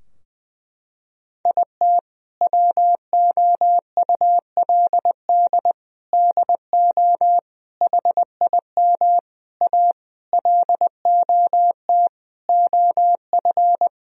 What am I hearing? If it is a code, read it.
Morse code, heard as a keyed tone, about 20 wpm, 700 Hz. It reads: IT WOULD DO HIM A LOT OF
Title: Sentences from Top 1000 Words - Review Part 3 - 20wpm